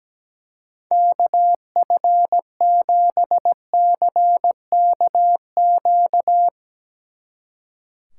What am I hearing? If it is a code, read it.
Morse code, heard as a keyed tone, about 17 wpm, 700 Hz. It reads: KF7CKQ
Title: Speed Racing - USA 2x3 Call Signs 17wpm